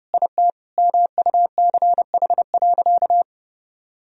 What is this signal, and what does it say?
Morse code, heard as a keyed tone, about 30 wpm, 700 Hz. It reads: IT MUCH.
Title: Sentences from Top 1000 Words - Review Part 1 - 30wpm